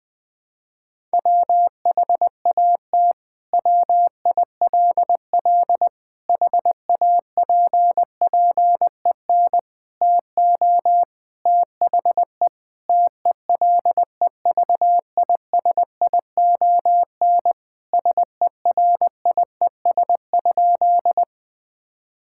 Morse code, heard as a keyed tone, about 20 wpm, 700 Hz. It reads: WHAT WILL HAPPEN TO THE TELEVISION SERIES?
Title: Sentences from Top 1000 Words - Review Part 5 - 20wpm